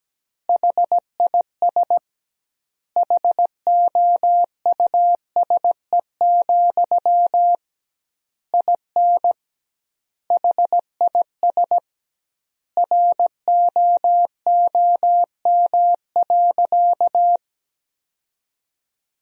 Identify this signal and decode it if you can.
Morse code, heard as a keyed tone, about 17 wpm, 700 Hz. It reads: HIS HOUSE, IN HIS ROOM.
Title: 2x Word-Spacing - Sentences from Top 300 Words - 17wpm